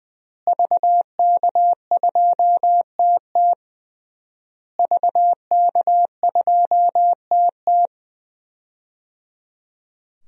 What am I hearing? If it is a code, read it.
Morse code, heard as a keyed tone, about 21 wpm, 700 Hz. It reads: VK2TT VK2TT
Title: DXCC-100 Call Signs - 5 Characters - Repeated 3x - 20wpm